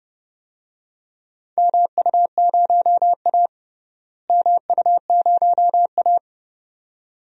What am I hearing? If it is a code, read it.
Morse code, heard as a keyed tone, about 29 wpm, 700 Hz. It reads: MU0A MU0A
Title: DXCC-100 Call Signs - 4 Characters - Repeated 2x - 30wpm